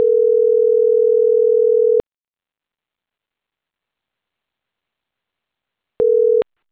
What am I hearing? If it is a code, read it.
Ringback tone (North America), 440 + 480 Hz.